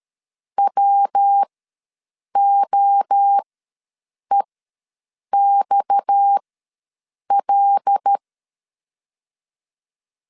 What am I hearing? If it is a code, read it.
Morse code, heard as a keyed tone, about 13 wpm, spaced as for 7 wpm, 800 Hz. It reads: WOEXL
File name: LEZIONI / 6 PKX0 / 6 PKX0 ws 12wpm 3448